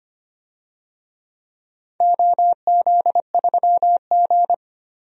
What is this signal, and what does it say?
Morse code, heard as a keyed tone, about 25 wpm, 700 Hz. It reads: OZ3G
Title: DXCC-100 Call Signs - 4 Characters - 25wpm